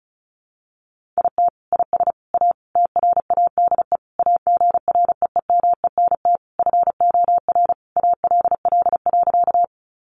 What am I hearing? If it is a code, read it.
Morse code, heard as a keyed tone, about 34 wpm, 700 Hz. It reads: IT IS A TRADE AGREEMENT FOR ALL.
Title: Sentences from Top 1500 Words 35wpm